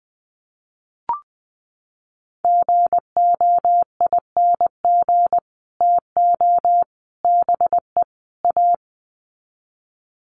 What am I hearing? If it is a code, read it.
Morse code, heard as a keyed tone, about 20 wpm, 700 Hz. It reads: GOING TO BE A
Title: Sets of 4 Words 20wpm